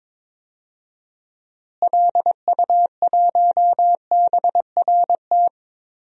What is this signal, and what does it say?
Morse code, heard as a keyed tone, about 22 wpm, 700 Hz. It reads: LU1BRT
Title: DXCC-100 Call Signs - 6 Characters - Repeated 3x - 22wpm